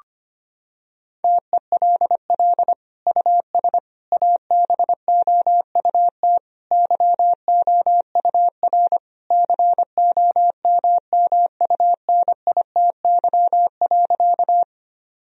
Morse code, heard as a keyed tone, about 25 wpm, 700 Hz. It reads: TELL US ABOUT YOUR COMMUNITY.
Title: Sentences from Top 1000 Words - Review Part 4 - 25wpm